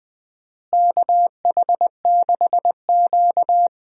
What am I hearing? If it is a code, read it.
Morse code, heard as a keyed tone, about 21 wpm, 700 Hz. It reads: KH6Q
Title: DXCC-100 Call Signs - 4 Characters - Repeated 3x - 20wpm